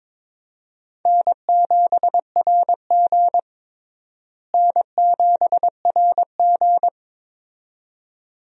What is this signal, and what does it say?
Morse code, heard as a keyed tone, about 21 wpm, 700 Hz. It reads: N7RG N7RG
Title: USA Call Signs - 4 Characters - Repeated 2x - 22wpm